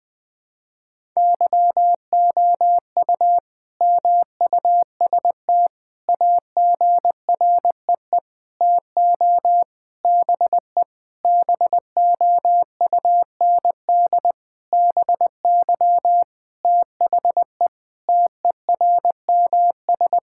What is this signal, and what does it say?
Morse code, heard as a keyed tone, about 20 wpm, 700 Hz. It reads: YOU MUST AGREE TO BE BOUND BY THE TERMS
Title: Sentences with 9-Word Phrases - Spoken Beginning - 20wpm